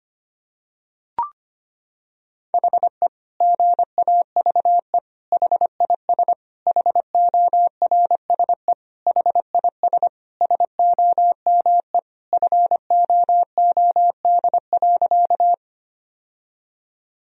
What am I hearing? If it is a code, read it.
Morse code, heard as a keyed tone, about 25 wpm, 700 Hz. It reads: HE GAVE HIS HORSE HIS SOME FOOD.
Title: Sentences from Top 1000 Words - Review Part 1 - 25wpm